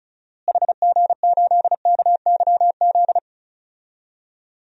Morse code, heard as a keyed tone, about 34 wpm, 700 Hz. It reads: HG8KYZ
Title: DXCC-100 Call Signs - 6 Characters - Repeated 2x - 35wpm